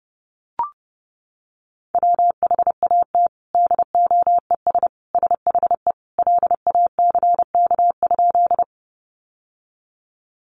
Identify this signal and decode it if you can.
Morse code, heard as a keyed tone, about 30 wpm, 700 Hz. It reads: WHAT DOES SHE LACK?
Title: Sentences from Top 1100 Words 30wpm